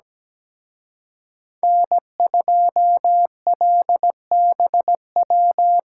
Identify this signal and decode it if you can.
Morse code, heard as a keyed tone, about 20 wpm, 700 Hz. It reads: N2LBW
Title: Speed Racing - USA 1x3 Call Signs 17wpm